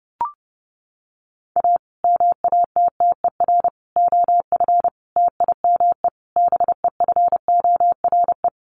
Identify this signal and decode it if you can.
Morse code, heard as a keyed tone, about 28 wpm, 700 Hz. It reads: A MATTER OF TIME BEFORE
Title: Sets of 5 Words Encore 30wpm